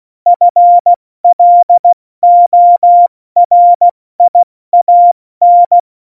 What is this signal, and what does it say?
Morse code, heard as a keyed tone, about 16 wpm, 700 Hz. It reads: FLORIAN
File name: Morsekurs / Witze / Witz 02. Juli 2019, 70 F80